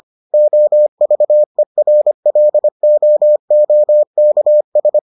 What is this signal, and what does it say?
Morse code, heard as a keyed tone, about 25 wpm, 600 Hz. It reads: OVERLOOKS